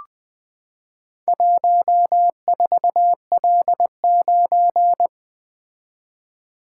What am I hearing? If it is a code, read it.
Morse code, heard as a keyed tone, about 20 wpm, 700 Hz. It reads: 14L9
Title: Four Letter-Number Combinations - 20wpm